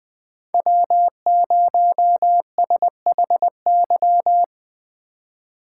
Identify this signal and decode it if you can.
Morse code, heard as a keyed tone, about 20 wpm, 700 Hz. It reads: W0SHY